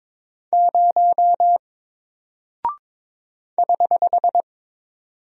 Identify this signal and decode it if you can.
Morse code, heard as a keyed tone, about 22 wpm, 700 Hz. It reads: 0 <HH>
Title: Single Character (Letter:Number:Punctuation:Prosign) Alt 22wpm